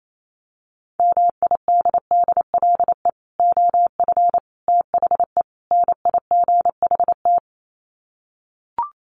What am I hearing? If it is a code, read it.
Morse code, heard as a keyed tone, about 28 wpm, 700 Hz. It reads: MIDDLE OF THE NIGHT
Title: Sets of 4 Words 28wpm